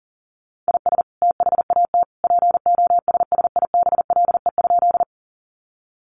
Morse code, heard as a keyed tone, about 40 wpm, 700 Hz. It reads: IS THAT POSSIBLE?
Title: Sentences from Top 500 Words 40wpm